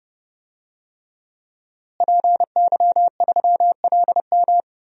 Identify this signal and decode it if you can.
Morse code, heard as a keyed tone, about 30 wpm, 700 Hz. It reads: PY3LM
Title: DXCC-100 Call Signs - 5 Characters - Repeated 3x - 30wpm